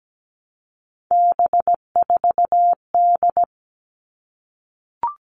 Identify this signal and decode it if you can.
Morse code, heard as a keyed tone, about 17 wpm, 700 Hz. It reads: B4D
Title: All Cut Numbers - 3-Digit 17wpm